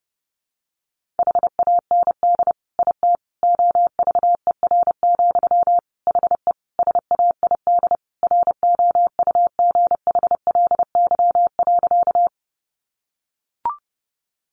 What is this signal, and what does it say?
Morse code, heard as a keyed tone, about 30 wpm, 700 Hz. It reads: HAND IT OVER, HE SAID ROUGHLY.
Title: Sentences from Top 1500 Words 30wpm